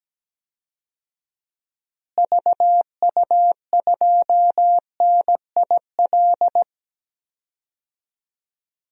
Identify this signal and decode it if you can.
Morse code, heard as a keyed tone, about 17 wpm, 700 Hz. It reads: VU2NIL
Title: DXCC-100 Call Signs - 6 Characters - Repeated 2x - 17wpm